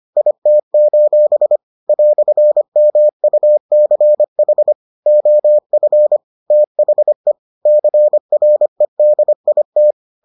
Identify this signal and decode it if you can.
Morse code, heard as a keyed tone, about 25 wpm, 600 Hz. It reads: IT: "MUCH OF THE CREDIT